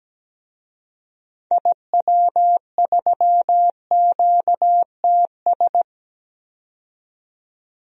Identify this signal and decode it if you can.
Morse code, heard as a keyed tone, about 17 wpm, 700 Hz. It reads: IW3QTS
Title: DXCC-100 Call Signs - 6 Characters - 17wpm